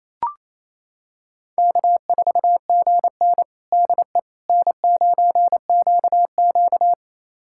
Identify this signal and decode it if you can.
Morse code, heard as a keyed tone, about 28 wpm, 700 Hz. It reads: K4GN DE N9QQ
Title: USA Callsign DE Callsign - 4 Characters - 28wpm